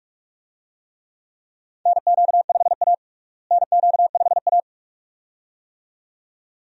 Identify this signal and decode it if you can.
Morse code, heard as a keyed tone, about 42 wpm, 700 Hz. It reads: NQ5A NQ5A
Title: USA Call Signs - 4 Characters - Repeated 2x - 45wpm